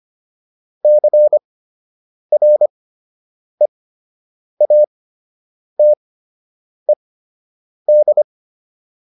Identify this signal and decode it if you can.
Morse code, heard as a keyed tone, about 25 wpm, spaced as for 8 wpm, 600 Hz. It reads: CREATED